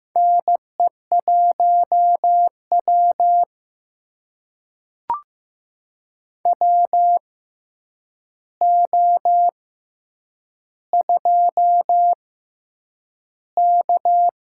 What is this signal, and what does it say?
Morse code, heard as a keyed tone, about 15 wpm, spaced as for 8 wpm, 700 Hz. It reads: NE1W WO2K
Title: USA Call Signs - 4 Characters - 6x Character Spacing - 15wpm